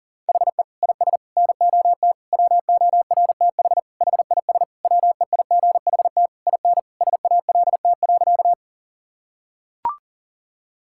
Morse code, heard as a keyed tone, about 40 wpm, 700 Hz. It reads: HE IS NOT WORTH HIS WEIGHT IN SALT.
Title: Sentences from Top 1000 Words - Review Part 5 - 40wpm